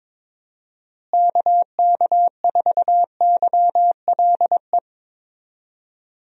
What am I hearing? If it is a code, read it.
Morse code, heard as a keyed tone, about 22 wpm, 700 Hz. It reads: KK4YLE
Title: USA Call Signs - 6 Characters - Repeated 2x - 22wpm